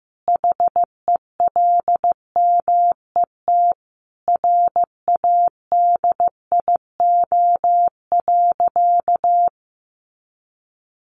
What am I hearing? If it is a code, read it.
Morse code, heard as a keyed tone, about 15 wpm, 700 Hz. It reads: HELMET RADIO.
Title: A Sound Of Thunder 15wpm